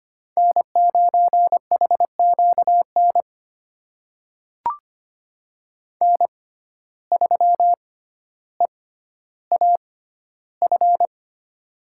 Morse code, heard as a keyed tone, about 25 wpm, spaced as for 13 wpm, 700 Hz. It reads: N9HQN N3EAF
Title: USA Call Signs - 6x Character Spacing - 25wpm